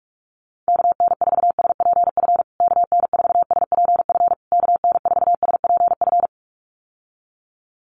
Morse code, heard as a keyed tone, about 44 wpm, 700 Hz. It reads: KN4SPF KN4SPF KN4SPF
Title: USA Call Signs Repeated 3x 45wpm